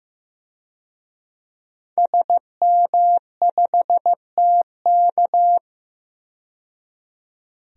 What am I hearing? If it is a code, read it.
Morse code, heard as a keyed tone, about 15 wpm, 700 Hz. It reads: SM5TK
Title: DXCC-100 Call Signs - 5 Characters - 15wpm